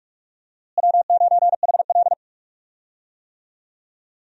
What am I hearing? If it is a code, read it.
Morse code, heard as a keyed tone, about 42 wpm, 700 Hz. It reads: W9HL
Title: USA Call Signs - 4 Characters - Repeated 2x - 45wpm